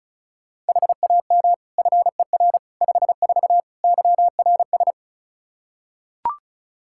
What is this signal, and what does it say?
Morse code, heard as a keyed tone, about 36 wpm, 700 Hz. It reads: HAM FER 54 YRS
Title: K1USN SST QSO Practice 35wpm